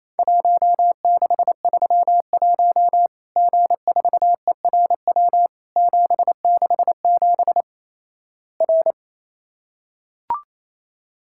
Morse code, heard as a keyed tone, about 28 wpm, 700 Hz. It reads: 1631 G4ERW 767 R
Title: Worked All Europe DX Contest - QTC Exchange - 28wpm